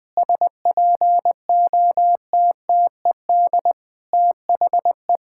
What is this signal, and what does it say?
Morse code, heard as a keyed tone, about 21 wpm, 700 Hz. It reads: SPOTTED THE
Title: Pangram Challenge v2 - 20wpm